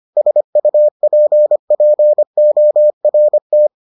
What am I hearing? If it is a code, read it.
Morse code, heard as a keyed tone, about 25 wpm, 600 Hz. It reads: SUPPORT